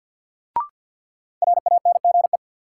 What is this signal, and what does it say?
Morse code, heard as a keyed tone, about 50 wpm, 700 Hz. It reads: RANGE
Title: Sets of 4 Words - New Word Drill - 50wpm